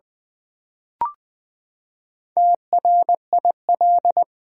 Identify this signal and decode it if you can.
Morse code, heard as a keyed tone, about 20 wpm, 700 Hz. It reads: TRIL
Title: Build A Word - 4 Letters At A Time - 8-Letter Words - 20wpm